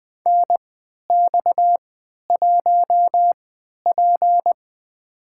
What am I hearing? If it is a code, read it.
Morse code, heard as a keyed tone, about 20 wpm, spaced as for 11 wpm, 700 Hz. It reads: NX1P
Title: USA Call Signs - 4 Characters - 3x Character Spacing - 20wpm